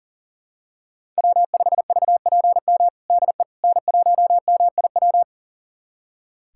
Morse code, heard as a keyed tone, about 40 wpm, 700 Hz. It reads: W5VPM DE N1MIW
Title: USA Callsign DE Callsign - 5 Characters - 40wpm